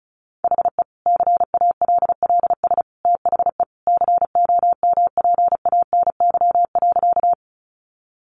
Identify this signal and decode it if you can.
Morse code, heard as a keyed tone, about 34 wpm, 700 Hz. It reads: HE CALLS THE COMPANY.
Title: Additional Easy Phrases 35wpm